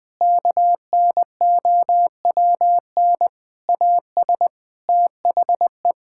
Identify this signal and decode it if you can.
Morse code, heard as a keyed tone, about 20 wpm, 700 Hz. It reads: KNOWN AS THE
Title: Sets of 8 Words - 20wpm